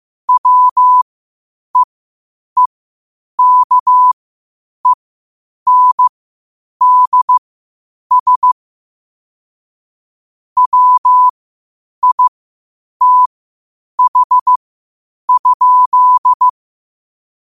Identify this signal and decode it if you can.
Morse code, heard as a keyed tone, about 15 wpm, spaced as for 8 wpm, 1000 Hz. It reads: WEEKENDS WITH?
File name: qotdfiles / 8wpm